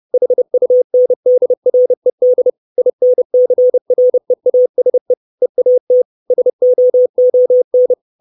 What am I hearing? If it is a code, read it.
Morse code, heard as a keyed tone, about 30 wpm, 500 Hz. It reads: HUNDRED INCREASE EAT SOON
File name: 632 Words 30WPM